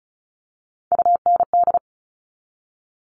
Morse code, heard as a keyed tone, about 36 wpm, 690 Hz. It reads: UND
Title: Top Three Letter Combinations in Words 35wpm